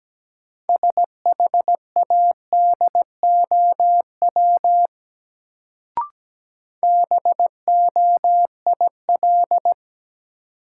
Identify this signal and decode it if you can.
Morse code, heard as a keyed tone, about 17 wpm, 700 Hz. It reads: SHADOW BOIL
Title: Sets of 5 Words Encore - New Word Drill - 17wpm